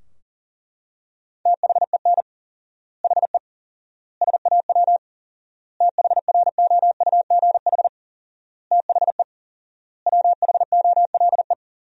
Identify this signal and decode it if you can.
Morse code, heard as a keyed tone, about 40 wpm, 700 Hz. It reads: THEN HE SAW THROUGH THE WHOLE
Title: 4x Word-Spacing - Sentences from Top 400 Words - 40wpm